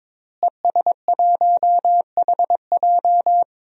Morse code, heard as a keyed tone, about 22 wpm, 700 Hz. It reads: ES1HJ
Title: DXCC-100 Call Signs - 5 Characters - Repeated 3x - 22wpm